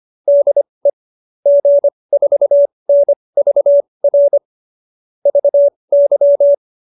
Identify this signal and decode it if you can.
Morse code, heard as a keyed tone, about 25 wpm, spaced as for 20 wpm, 570 Hz. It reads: DE G4NVR VY